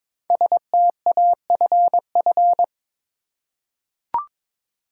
Morse code, heard as a keyed tone, about 22 wpm, 700 Hz. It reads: STAFF
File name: 5 Letter Words - First 3 Letters Given - 22wpm